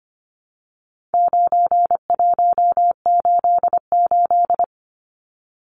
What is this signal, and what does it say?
Morse code, heard as a keyed tone, about 25 wpm, 700 Hz. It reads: 9188
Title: Numbers - 4-Digit 25wpm